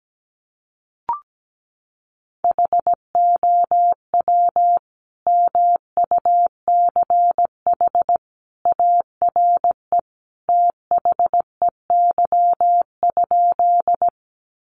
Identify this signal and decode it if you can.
Morse code, heard as a keyed tone, about 17 wpm, 700 Hz. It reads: HOW MUCH ARE THEY?
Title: Sentences from Top 200 Words 17wpm